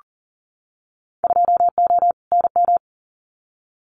Morse code, heard as a keyed tone, about 38 wpm, 700 Hz. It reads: 2O NM
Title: Winter Field Day Exchanges 40wpm